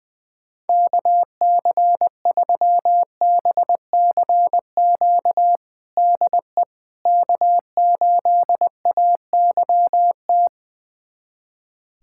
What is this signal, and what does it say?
Morse code, heard as a keyed tone, about 20 wpm, 700 Hz. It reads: KC3BCQ DE K8AYT
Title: USA Callsign DE Callsign 20wpm